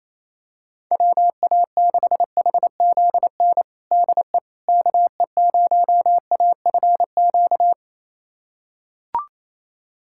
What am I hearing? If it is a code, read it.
Morse code, heard as a keyed tone, about 27 wpm, 700 Hz. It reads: WA6HZN DE KE0AFQ
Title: USA Callsign DE Callsign - 6 Characters - 28wpm